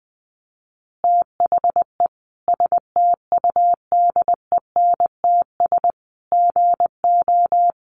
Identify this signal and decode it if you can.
Morse code, heard as a keyed tone, about 20 wpm, 700 Hz. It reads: THE STUDENTS GO
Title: Easy Phrases 20wpm